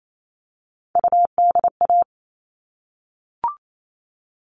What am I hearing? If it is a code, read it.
Morse code, heard as a keyed tone, about 28 wpm, 700 Hz. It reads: UDA
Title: All Cut Numbers - 3-Digit 28wpm